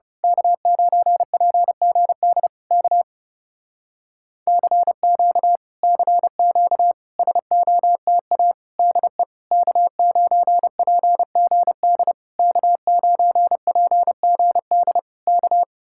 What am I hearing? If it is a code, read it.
Morse code, heard as a keyed tone, about 28 wpm, 700 Hz. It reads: K9PGD K CQ CQ SOTA DE K9PGD K9PGD K
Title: Speed-Racing Common QSO Phrases Encore - 30wpm